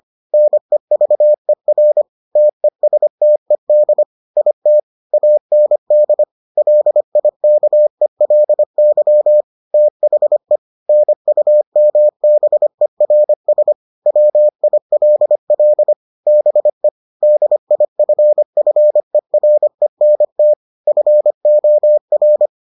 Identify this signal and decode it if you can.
Morse code, heard as a keyed tone, about 25 wpm, 600 Hz. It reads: NEVER TESTED IT AND LIKELY THE NUMBERS WILL BE DIFFERENT FOR